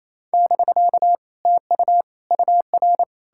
Morse code, heard as a keyed tone, about 38 wpm, spaced as for 33 wpm, 700 Hz. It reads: <BK> TU UR
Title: Speed-Racing Common QSO Phrases - 28wpm